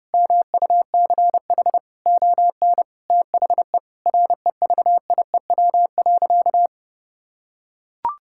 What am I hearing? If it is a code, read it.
Morse code, heard as a keyed tone, about 31 wpm, 700 Hz. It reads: MUCH ON THE REVIEW.
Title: Sentences from Top 1000 Words - Review Part 6 - 30wpm